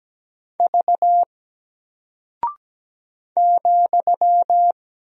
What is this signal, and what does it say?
Morse code, heard as a keyed tone, about 17 wpm, 700 Hz. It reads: V ,
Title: Single Character (Letter:Number:Punctuation:Prosign) 17wpm